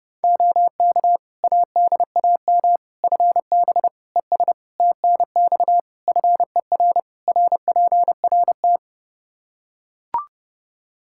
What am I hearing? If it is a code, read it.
Morse code, heard as a keyed tone, about 32 wpm, 700 Hz. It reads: OK ADAM FB ES TNX FER RPRT
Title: Practice QSO - Exchange 2 - Info - 30wpm